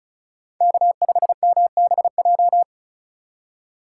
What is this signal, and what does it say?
Morse code, heard as a keyed tone, about 36 wpm, 700 Hz. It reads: K5MBJ
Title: USA Call Signs - 5 Characters - 2x Character Spacing - 35wpm